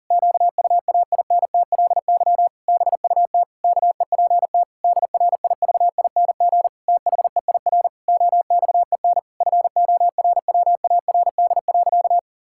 Morse code, heard as a keyed tone, about 40 wpm, 700 Hz. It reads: QUAINTLY BUT KEPT DRIVING THEIR OXEN FORWARD.
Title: Pangram Challenge v2 - 40wpm